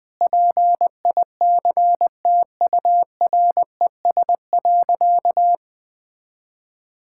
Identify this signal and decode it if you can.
Morse code, heard as a keyed tone, about 20 wpm, 700 Hz. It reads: PICTURES.
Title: Sentences with 8-Word Phrases - Spoken Beginning - 20wpm